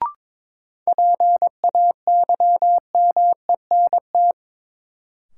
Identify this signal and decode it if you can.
Morse code, heard as a keyed tone, about 22 wpm, 700 Hz. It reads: PAYMENT
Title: Sentences from Top 1200 Words - New Word Drill Alt - 22wpm